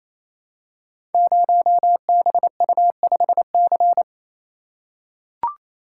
Morse code, heard as a keyed tone, about 29 wpm, 700 Hz. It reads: 0BU5C
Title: Code Groups - 5 Letter:Numbers - 28wpm